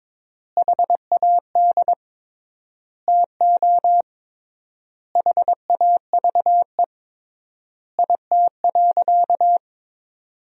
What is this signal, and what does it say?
Morse code, heard as a keyed tone, about 22 wpm, 700 Hz. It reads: HAD TO HAVE IT.
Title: 3x Word-Spacing - Sentences from Top 100 Words - 22wpm